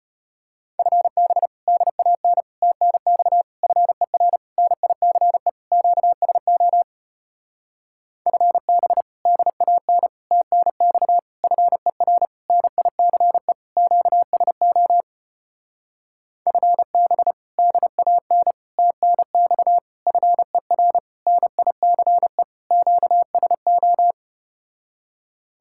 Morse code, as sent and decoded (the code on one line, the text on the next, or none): ..-. -... / -.. .- -. / - -. -..- / ..-. .-. / -. .. -.-. / --.- ... --- / ..-. -... / -.. .- -. / - -. -..- / ..-. .-. / -. .. -.-. / --.- ... --- / ..-. -... / -.. .- -. / - -. -..- / ..-. .-. / -. .. -.-. / --.- ... ---
FB DAN TNX FER NICE QSO FB DAN TNX FER NICE QSO FB DAN TNX FER NICE QSO